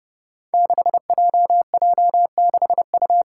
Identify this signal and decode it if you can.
Morse code, heard as a keyed tone, about 30 wpm, 700 Hz. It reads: 6JJ6U
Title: Five Challenging Letter-Number Combinations 30wpm